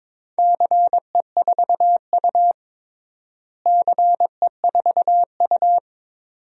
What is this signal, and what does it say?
Morse code, heard as a keyed tone, about 22 wpm, 700 Hz. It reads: CE4U CE4U
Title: DXCC-100 Call Signs - 4 Characters - Repeated 2x - 22wpm